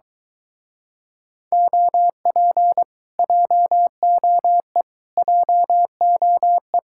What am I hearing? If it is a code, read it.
Morse code, heard as a keyed tone, about 22 wpm, 700 Hz. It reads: OP JOE JOE